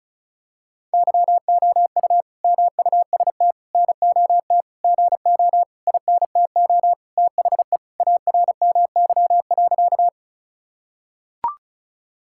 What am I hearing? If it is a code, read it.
Morse code, heard as a keyed tone, about 36 wpm, 700 Hz. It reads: YOU MUST NOT GO INTO THE ARMY.
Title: Sentences from Top 1000 Words - Review Part 5 - 35wpm